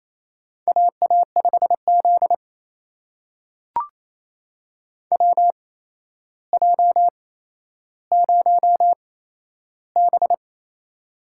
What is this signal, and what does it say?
Morse code, heard as a keyed tone, about 28 wpm, spaced as for 12 wpm, 700 Hz. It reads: AA5Z WJ0B
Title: USA Call Signs - 4 Characters - 8x Character Spacing - 28wpm